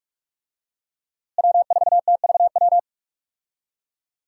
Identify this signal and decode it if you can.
Morse code, heard as a keyed tone, about 45 wpm, 700 Hz. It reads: W4TVW
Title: USA Call Signs - 5 Characters - Repeated 3x - 45wpm